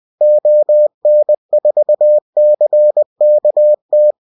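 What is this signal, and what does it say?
Morse code, heard as a keyed tone, about 20 wpm, 600 Hz. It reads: ON4CKT
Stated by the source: Amateur station CQ call in Morse.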